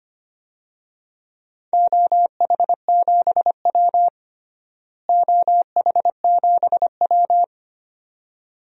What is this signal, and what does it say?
Morse code, heard as a keyed tone, about 25 wpm, 700 Hz. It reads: OH7W OH7W
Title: DXCC-100 Call Signs - 4 Characters - Repeated 2x - 25wpm